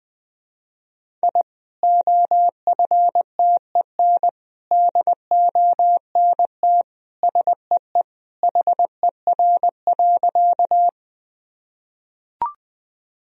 Morse code, heard as a keyed tone, about 20 wpm, 700 Hz. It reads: I OFTEN DONT SEE HER.
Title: Sentences from Top 1000 Words - Review Part 1 - 20wpm